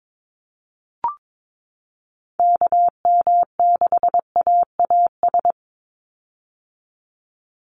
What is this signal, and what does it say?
Morse code, heard as a keyed tone, about 22 wpm, 700 Hz. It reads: KM6AAS
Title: USA Call Signs 22wpm